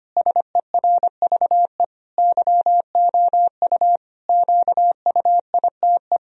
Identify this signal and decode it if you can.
Morse code, heard as a keyed tone, about 25 wpm, 700 Hz. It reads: SERVE YOU QUITE
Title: Sentences from Top 1000 Words - Review Part 2 - 25wpm